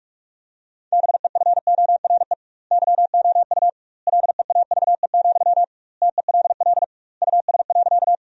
Morse code, heard as a keyed tone, about 46 wpm, 700 Hz. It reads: BEFORE YOU LEAVE, TELL US.
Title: 2x Word-Spacing - Sentences from Top 400 Words - 45wpm